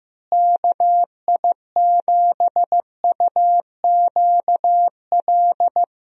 Morse code, heard as a keyed tone, about 15 wpm, 700 Hz. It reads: KI7UQL
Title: USA Call Signs - 6 Characters - Repeated 2x - 15wpm